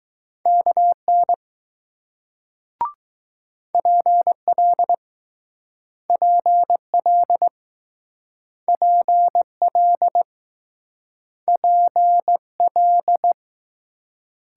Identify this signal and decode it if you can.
Morse code, heard as a keyed tone, about 20 wpm, 700 Hz. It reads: KN PL PL PL P L